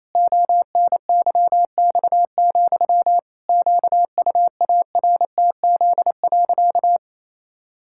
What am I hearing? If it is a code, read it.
Morse code, heard as a keyed tone, about 28 wpm, 700 Hz. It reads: ONYX, QUARTZ.
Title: Pangram Challenge v2 - 28wpm